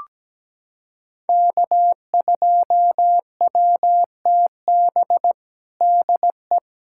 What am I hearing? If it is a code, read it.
Morse code, heard as a keyed tone, about 17 wpm, 700 Hz. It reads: K2WTB DE
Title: USA Callsign DE Callsign - 5 Characters - 17wpm